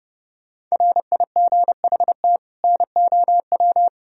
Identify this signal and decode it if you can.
Morse code, heard as a keyed tone, about 31 wpm, 700 Hz. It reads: RIGHT NOW